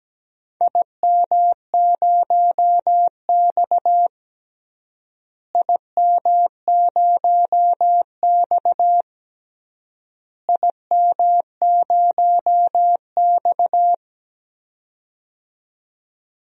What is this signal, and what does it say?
Morse code, heard as a keyed tone, about 17 wpm, 700 Hz. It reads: IM0X IM0X IM0X
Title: DXCC-100 Call Signs - 4 Characters - Repeated 3x - 17wpm